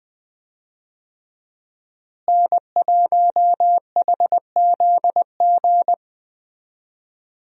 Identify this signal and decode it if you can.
Morse code, heard as a keyed tone, about 20 wpm, 700 Hz. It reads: N1HZG
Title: USA Call Signs - 5 Characters - Repeated 3x - 20wpm